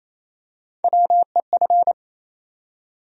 Morse code, heard as a keyed tone, about 28 wpm, 700 Hz. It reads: WEF
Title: USA Call Sign Postfixes - 3 Letters 28wpm